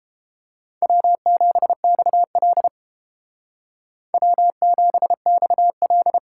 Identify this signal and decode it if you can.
Morse code, heard as a keyed tone, about 29 wpm, 700 Hz. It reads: W7XL W7XL